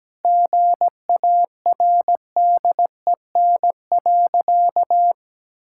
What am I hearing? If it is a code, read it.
Morse code, heard as a keyed tone, about 17 wpm, 700 Hz. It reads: GARDEN.
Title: Sentences from Top 1000 Words - Review Part 5 - 17wpm